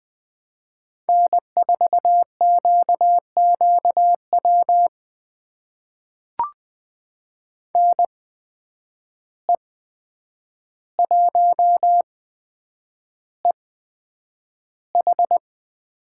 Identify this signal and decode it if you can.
Morse code, heard as a keyed tone, about 20 wpm, spaced as for 9 wpm, 700 Hz. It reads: N4QQW NE1EH
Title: USA Call Signs - 5 Characters - 8x Character Spacing - 20wpm